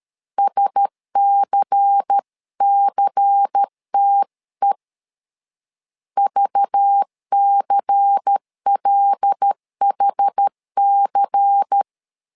Morse code, heard as a keyed tone, about 13 wpm, 800 Hz. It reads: SCCTE VCLHC
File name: LEZIONI / 5 LBVC / 5 LBVC 12wpm 1955